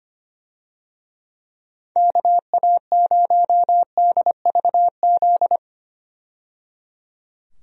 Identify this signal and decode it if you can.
Morse code, heard as a keyed tone, about 25 wpm, 700 Hz. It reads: KA0DVZ
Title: USA Call Signs - 6 Characters - 25wpm